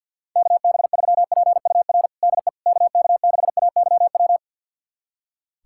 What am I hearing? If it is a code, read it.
Morse code, heard as a keyed tone, about 46 wpm, 700 Hz. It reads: KB3PUR DE KK6AYW
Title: USA Callsign DE Callsign - 6 Characters - 50wpm